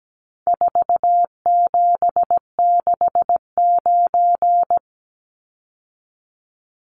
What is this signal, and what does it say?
Morse code, heard as a keyed tone, about 17 wpm, 700 Hz. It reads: 4769
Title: Numbers - 4-Digit 17wpm